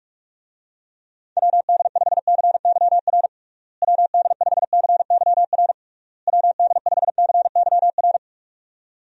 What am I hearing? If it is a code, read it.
Morse code, heard as a keyed tone, about 46 wpm, 700 Hz. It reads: WD5CYR WD5CYR WD5CYR
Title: USA Call Signs - 6 Characters - Repeated 3x - 45wpm